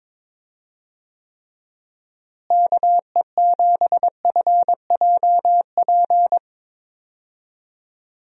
Morse code, heard as a keyed tone, about 22 wpm, 700 Hz. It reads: KE7FJP
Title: USA Call Signs - 6 Characters - Repeated 3x - 22wpm